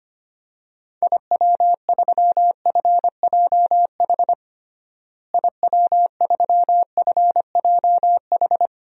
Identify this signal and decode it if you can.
Morse code, heard as a keyed tone, about 25 wpm, 700 Hz. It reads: IW3FJH IW3FJH
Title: DXCC-100 Call Signs - 6 Characters - Repeated 3x - 25wpm